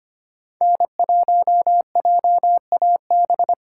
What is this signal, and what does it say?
Morse code, heard as a keyed tone, about 32 wpm, 700 Hz. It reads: N1JAB